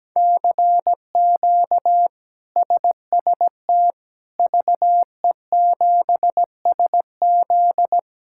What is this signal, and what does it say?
Morse code, heard as a keyed tone, about 17 wpm, 700 Hz. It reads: CQ SST VE7SZ
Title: K1USN SST QSO Practice - 17wpm